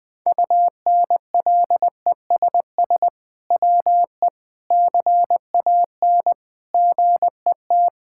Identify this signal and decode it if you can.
Morse code, heard as a keyed tone, about 20 wpm, 700 Hz. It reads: UNLESS WE CAN GET
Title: Sentences from Top 1000 Words - Review Part 6 - 20wpm